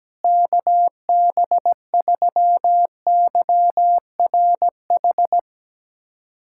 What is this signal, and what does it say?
Morse code, heard as a keyed tone, about 17 wpm, 700 Hz. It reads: KB3YRH
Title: USA Callsign DE Callsign 17wpm